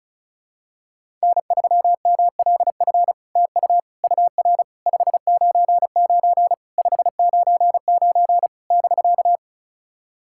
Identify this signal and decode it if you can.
Morse code, heard as a keyed tone, about 36 wpm, 700 Hz. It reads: N3MLF TU UR 599 599 <BK>
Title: SOTA QSO Practice 35wpm